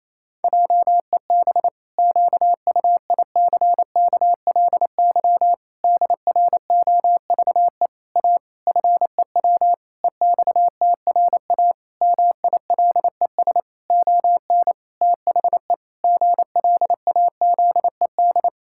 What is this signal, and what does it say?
Morse code, heard as a keyed tone, about 27 wpm, 700 Hz. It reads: JEB QUICKLY DROVE A FEW EXTRA MILES ON THE GLAZED
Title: Pangram Challenge v2 - 28wpm